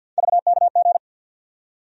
Morse code, heard as a keyed tone, about 52 wpm, 700 Hz. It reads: VKG
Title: USA Call Sign Postfixes - 3 Letters 50wpm